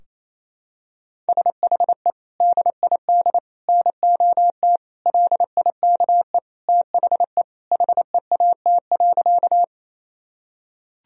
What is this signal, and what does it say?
Morse code, heard as a keyed tone, about 28 wpm, 700 Hz. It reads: SHE DID NOT LIKE THE HEAT.
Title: Sentences from Top 1000 Words - Review Part 3 - 28wpm